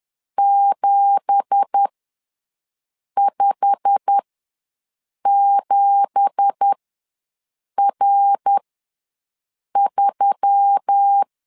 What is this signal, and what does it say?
Morse code, heard as a keyed tone, about 10 wpm, spaced as for 6 wpm, 800 Hz. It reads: 757R3
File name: LEZIONI / 9 2867 ws 10wpm 4215